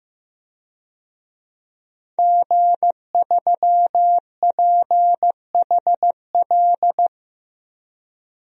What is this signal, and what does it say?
Morse code, heard as a keyed tone, about 15 wpm, 700 Hz. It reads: G3PHL
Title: DXCC-100 Call Signs - 5 Characters - 15wpm